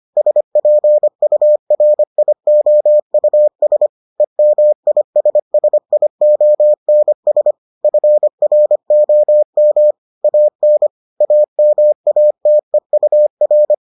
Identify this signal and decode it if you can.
Morse code, heard as a keyed tone, about 25 wpm, 600 Hz. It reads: SPURIOUS EMISSIONS FROM AN AMATEUR